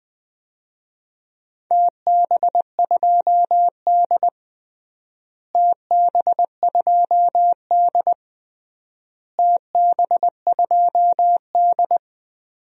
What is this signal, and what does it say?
Morse code, heard as a keyed tone, about 20 wpm, 700 Hz. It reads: TB2D TB2D TB2D
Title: DXCC-100 Call Signs - 4 Characters - Repeated 3x - 20wpm